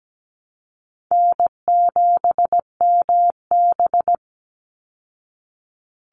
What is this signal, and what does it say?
Morse code, heard as a keyed tone, about 17 wpm, 700 Hz. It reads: N7MB
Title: Easy Call Signs Repeated 3x - 17wpm